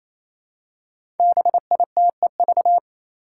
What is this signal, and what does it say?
Morse code, heard as a keyed tone, about 27 wpm, 700 Hz. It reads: BITEV
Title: Five Challenging Letter-Number Combinations 28wpm